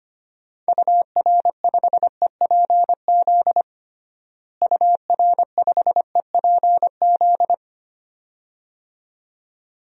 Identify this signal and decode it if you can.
Morse code, heard as a keyed tone, about 25 wpm, 700 Hz. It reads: UR5EPZ UR5EPZ
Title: DXCC-100 Call Signs - 6 Characters - Repeated 2x - 25wpm